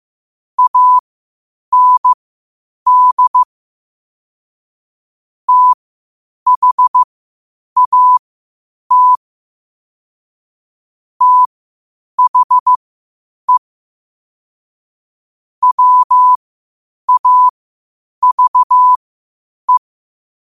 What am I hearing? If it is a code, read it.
Morse code, heard as a keyed tone, about 15 wpm, spaced as for 8 wpm, 1000 Hz. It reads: AND THAT THE WAVE
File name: qotdfiles / 8wpm